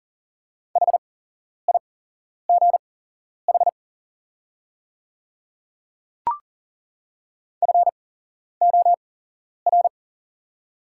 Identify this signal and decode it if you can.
Morse code, heard as a keyed tone, about 41 wpm, spaced as for 11 wpm, 690 Hz. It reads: HIGH FOR
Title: 8x Character Spacing - Top 200 Words - 40wpm